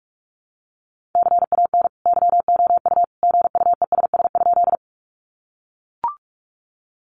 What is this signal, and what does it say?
Morse code, heard as a keyed tone, about 44 wpm, 700 Hz. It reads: CAN YOU GUESS?
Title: Sentences from Top 1500 Words 45wpm